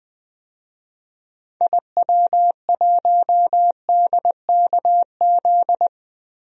Morse code, heard as a keyed tone, about 20 wpm, 700 Hz. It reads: IW1DKZ
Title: DXCC-100 Call Signs - 6 Characters - Repeated 3x - 20wpm